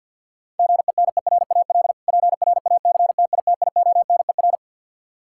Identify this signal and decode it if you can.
Morse code, heard as a keyed tone, about 52 wpm, 700 Hz. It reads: GENERAL PRACTITIONER
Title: Practice QSO - Exchange 3 - More Info - 50wpm